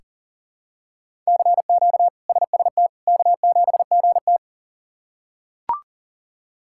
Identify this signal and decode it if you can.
Morse code, heard as a keyed tone, about 40 wpm, 700 Hz. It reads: CQ SST K7GT
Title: K1USN SST QSO Practice - 40wpm